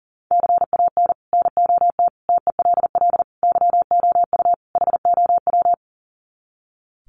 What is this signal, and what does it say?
Morse code, heard as a keyed tone, about 40 wpm, 700 Hz. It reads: CAN NOT TELL YOU HOW
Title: Sets of 5 Words Encore2 40wpm